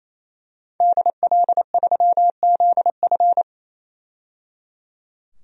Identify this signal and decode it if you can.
Morse code, heard as a keyed tone, about 28 wpm, 700 Hz. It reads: DL3ZF
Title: DXCC-100 Call Signs - 5 Characters - 28wpm